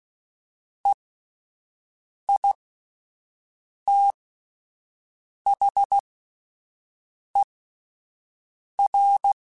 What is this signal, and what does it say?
Morse code, heard as a keyed tone, about 16 wpm, spaced as for 5 wpm, 800 Hz. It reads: EITHER
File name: Spaced / Chuckles Spaced-mp3s / Squeaky sp @10